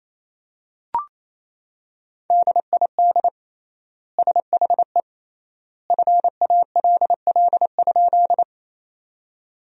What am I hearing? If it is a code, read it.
Morse code, heard as a keyed tone, about 28 wpm, 700 Hz. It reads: DID SHE FALL?
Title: 3x Word-Spacing - Sentences from Top 500 Words - 28wpm